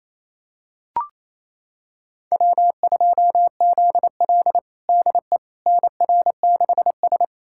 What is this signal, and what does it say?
Morse code, heard as a keyed tone, about 28 wpm, 700 Hz. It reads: W2ZL DE NR6S
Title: USA Callsign DE Callsign - 4 Characters - 28wpm